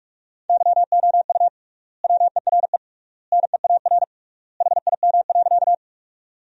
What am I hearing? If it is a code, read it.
Morse code, heard as a keyed tone, about 43 wpm, 700 Hz. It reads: YOU WERE NEAR HIM.
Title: 3x Word-Spacing - Sentences from Top 200 Words - 45wpm